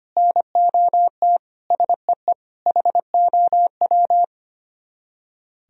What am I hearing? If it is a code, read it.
Morse code, heard as a keyed tone, about 33 wpm, 700 Hz. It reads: NOT SEE HOW